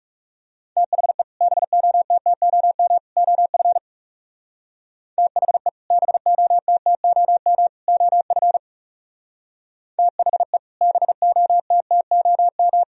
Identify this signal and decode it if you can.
Morse code, heard as a keyed tone, about 37 wpm, 700 Hz. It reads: THE BOTTOM OF THE BOTTOM OF THE BOTTOM
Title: Speed-Racing - Sets of 3 Words 30wpm